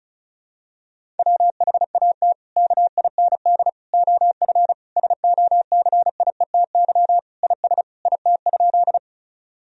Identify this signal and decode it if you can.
Morse code, heard as a keyed tone, about 36 wpm, 700 Hz. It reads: WHAT KIND OF SOCIETY IS IT?
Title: Sentences from Top 1000 Words - Review Part 4 - 35wpm